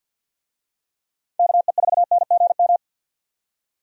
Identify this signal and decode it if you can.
Morse code, heard as a keyed tone, about 49 wpm, 700 Hz. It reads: KE4NGM
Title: USA Call Signs - 6 Characters - 50wpm